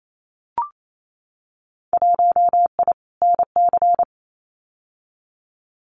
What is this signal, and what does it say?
Morse code, heard as a keyed tone, about 28 wpm, 700 Hz. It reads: 1I NC
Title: Winter Field Day Exchanges Encore 28wpm